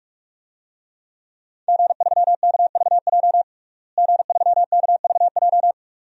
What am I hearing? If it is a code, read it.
Morse code, heard as a keyed tone, about 45 wpm, 700 Hz. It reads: G3KVJ G3KVJ
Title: DXCC-100 Call Signs - 5 Characters - Repeated 2x - 45wpm